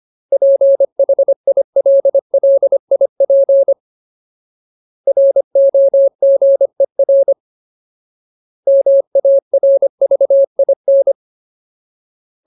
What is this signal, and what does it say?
Morse code, heard as a keyed tone, about 25 wpm, 550 Hz. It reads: PHILLIP ROGER MARVIN
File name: Ultimate Word List / 25WPM Ultimate